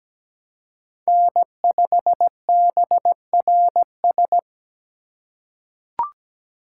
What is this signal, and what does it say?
Morse code, heard as a keyed tone, about 17 wpm, 700 Hz. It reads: N5BRS
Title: USA Call Signs - 5 Characters - 4x Character Spacing - 17wpm